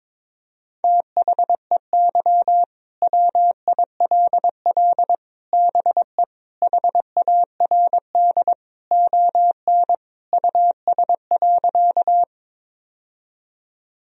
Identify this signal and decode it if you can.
Morse code, heard as a keyed tone, about 22 wpm, 700 Hz. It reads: THEY WILL BE HARD ON US.
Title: Sentences from Top 1000 Words - Review Part 1 - 22wpm